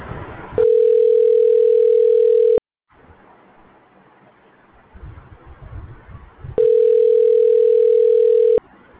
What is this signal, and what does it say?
Ringback tone (North America), 440 + 480 Hz.